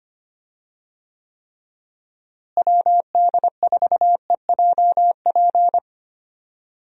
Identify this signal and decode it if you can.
Morse code, heard as a keyed tone, about 25 wpm, 700 Hz. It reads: WD4EJP
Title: USA Call Signs - 6 Characters - Repeated 3x - 25wpm